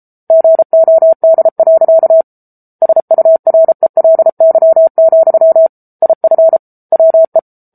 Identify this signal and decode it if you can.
Morse code, heard as a keyed tone, about 34 wpm, 650 Hz. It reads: GOD. SURELY, IF WE
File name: Book Two / 07-Chapter 7 - 33 WPM